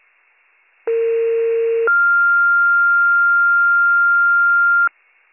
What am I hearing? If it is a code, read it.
Two-tone page, heard as a one-second tone, then a three-second tone, about 460 then 1400 Hz. It is heard from 0.9 to 4.9 s.